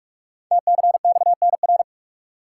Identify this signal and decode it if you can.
Morse code, heard as a keyed tone, about 44 wpm, 700 Hz. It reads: TCXNR
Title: Code Groups - 5 Letters - 45wpm